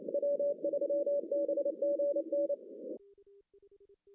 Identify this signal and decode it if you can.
Morse code, heard as a keyed tone, about 28 wpm, 570 Hz, with noise. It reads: W3BGN
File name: CW / cqww-cw-k5tr